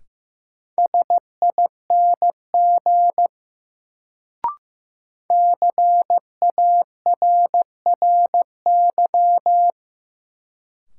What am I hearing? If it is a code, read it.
Morse code, heard as a keyed tone, about 15 wpm, 700 Hz. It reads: SING CARRY
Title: Sentences from Top 400 Words - New Word Drill - 15wpm